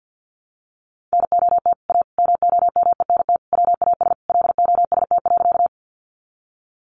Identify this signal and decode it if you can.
Morse code, heard as a keyed tone, about 50 wpm, 700 Hz. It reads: NOT A MOMENT WAS LOST.
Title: Sentences from Top 1400 Words 50wpm